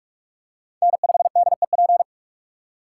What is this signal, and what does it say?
Morse code, heard as a keyed tone, about 44 wpm, 700 Hz. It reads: N5DEP